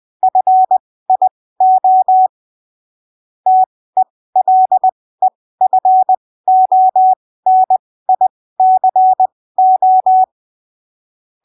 Morse code, heard as a keyed tone, about 20 wpm, spaced as for 15 wpm, 750 Hz. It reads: FIO TELEFONICO
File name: arquivos / texto / texto 13 20ppm